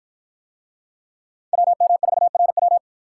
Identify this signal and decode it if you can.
Morse code, heard as a keyed tone, about 50 wpm, 700 Hz. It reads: WM4RW